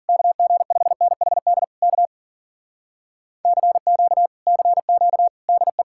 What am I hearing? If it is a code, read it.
Morse code, heard as a keyed tone, about 53 wpm, 700 Hz. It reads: KG5NHD K CQ CQ DE